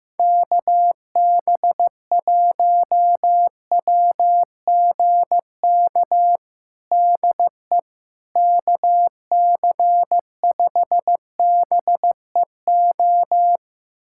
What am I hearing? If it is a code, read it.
Morse code, heard as a keyed tone, about 15 wpm, 700 Hz. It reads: KB1WGK DE KC5BEO
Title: USA Callsign DE Callsign - 6 Characters - 15wpm